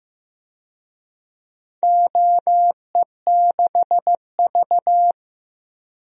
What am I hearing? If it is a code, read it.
Morse code, heard as a keyed tone, about 15 wpm, 700 Hz. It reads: OE6V
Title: DXCC-100 Call Signs - 4 Characters - 15wpm